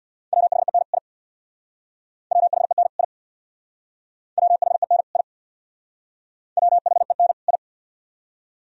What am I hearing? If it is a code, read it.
Morse code, heard as a keyed tone, about 63 wpm, 700 Hz. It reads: WHEN I WHEN I WHEN I WHEN I